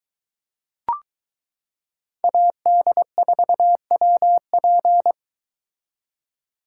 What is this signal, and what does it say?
Morse code, heard as a keyed tone, about 22 wpm, 700 Hz. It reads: AD4WP